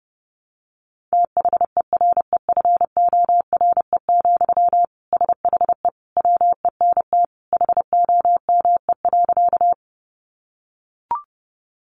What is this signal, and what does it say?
Morse code, heard as a keyed tone, about 30 wpm, 700 Hz. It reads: THEREFORE, SHE WENT HOME.
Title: Sentences from Top 800 Words 30wpm